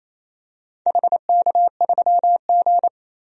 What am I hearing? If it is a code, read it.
Morse code, heard as a keyed tone, about 27 wpm, 700 Hz. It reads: HK3G
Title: DXCC-100 Call Signs - 4 Characters - Repeated 3x - 28wpm